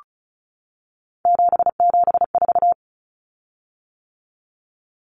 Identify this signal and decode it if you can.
Morse code, heard as a keyed tone, about 34 wpm, 700 Hz. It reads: 774
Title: Common Cut Numbers - 3-Digit 35wpm